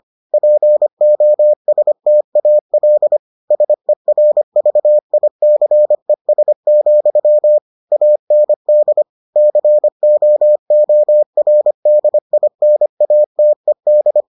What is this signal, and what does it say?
Morse code, heard as a keyed tone, about 25 wpm, 600 Hz. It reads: POSTAL SERVICES, AND COORDINATED